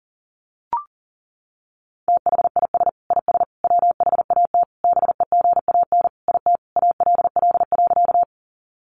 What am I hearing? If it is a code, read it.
Morse code, heard as a keyed tone, about 37 wpm, 700 Hz. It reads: THIS IS WHAT BEGAN IT ALL.
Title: Sentences from Top 300 Words 40wpm